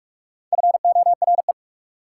Morse code, heard as a keyed tone, about 58 wpm, 690 Hz. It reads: FORE